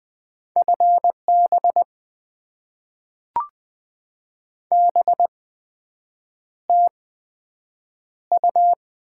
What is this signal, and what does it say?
Morse code, heard as a keyed tone, about 20 wpm, spaced as for 7 wpm, 700 Hz. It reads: FB BTU
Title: Top 20 QSO Elements - 8x Character Spacing - 20wpm